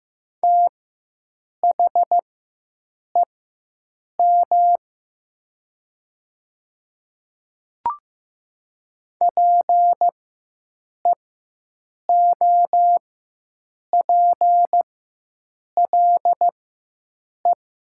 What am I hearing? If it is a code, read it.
Morse code, heard as a keyed tone, about 15 wpm, spaced as for 7 wpm, 700 Hz. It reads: THEM PEOPLE
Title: 4x Character Spacing - Top 200 Words - 15wpm